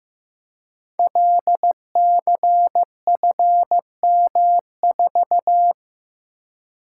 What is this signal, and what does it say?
Morse code, heard as a keyed tone, about 15 wpm, 700 Hz. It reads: LCFM4
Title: Code Groups - 5 Letter:Numbers - 15wpm